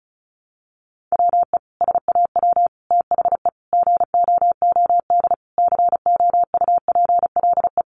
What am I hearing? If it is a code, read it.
Morse code, heard as a keyed tone, about 34 wpm, 700 Hz. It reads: WE SAW THE GOOD COUPLE
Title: Sentences from Top 1100 Words 35wpm